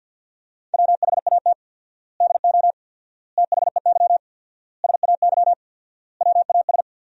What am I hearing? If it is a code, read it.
Morse code, heard as a keyed tone, about 50 wpm, 700 Hz. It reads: WHAT DO THEY SAY WAS
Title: 4x Word-Spacing - Sentences from Top 200 Words - 50wpm